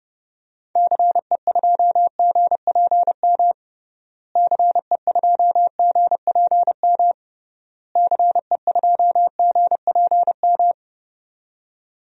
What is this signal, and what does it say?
Morse code, heard as a keyed tone, about 30 wpm, 700 Hz. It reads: CE2GPM CE2GPM CE2GPM
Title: DXCC-100 Call Signs - 6 Characters - Repeated 3x - 30wpm